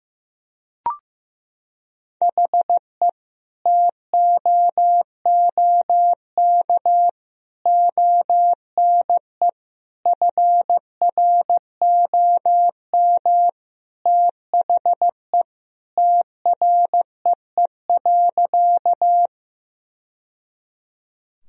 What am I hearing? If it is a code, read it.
Morse code, heard as a keyed tone, about 15 wpm, 700 Hz. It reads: HE TOOK ONE FROM THE TREE.
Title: Sentences from Top 1000 Words - Review Part 1 - 15wpm